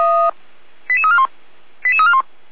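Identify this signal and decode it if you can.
Five-tone selective call (ZVEI), heard as quick runs of five tones, tones of about 70 ms. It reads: 88341 88341
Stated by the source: ZVEI five-tone call 88199 received by SDR on channel 455, recorded by fire brigade Feuerwehr Orlamünde.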